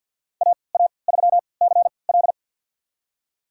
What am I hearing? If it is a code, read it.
Morse code, heard as a keyed tone, about 49 wpm, spaced as for 28 wpm, 700 Hz. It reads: AA3CL
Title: USA Call Signs - 5 Characters - 3x Character Spacing - 50wpm